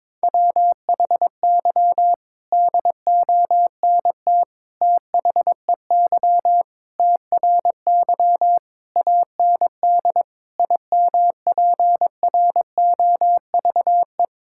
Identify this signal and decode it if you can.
Morse code, heard as a keyed tone, about 21 wpm, 700 Hz. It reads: WHY DONT THEY TRY AND IMPROVE
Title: Sentences from Top 1000 Words - Review Part 5 - 22wpm